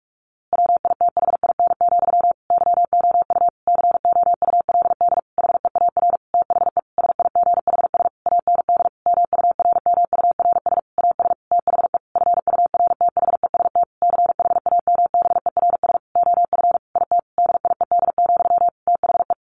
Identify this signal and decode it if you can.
Morse code, heard as a keyed tone, about 44 wpm, 700 Hz. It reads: WITHIN, YOU COULD HEAR THE SIGHS AND MURMURS AS THE FURTHEST CHAMBERS OF IT DIED, THE
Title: A Sound Of Thunder 45wpm